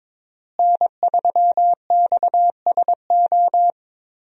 Morse code, heard as a keyed tone, about 22 wpm, 700 Hz. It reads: N3XSO
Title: USA Call Signs - 5 Characters - Repeated 3x - 22wpm